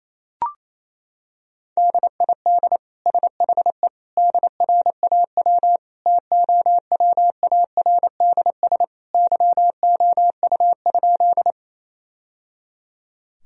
Morse code, heard as a keyed tone, about 28 wpm, 700 Hz. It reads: DID SHE DRAW TOWARDS YOU?
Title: Sentences from Top 1000 Words - Review Part 2 - 28wpm